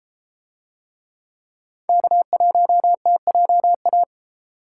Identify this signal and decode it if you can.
Morse code, heard as a keyed tone, about 32 wpm, 700 Hz. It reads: K1TJA